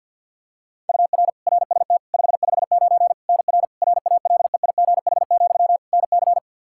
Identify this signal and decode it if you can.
Morse code, heard as a keyed tone, about 51 wpm, 700 Hz. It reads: UR RST 559 NR RALEIGH, NC
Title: Practice QSO - Exchange 1 - Beginning - 50wpm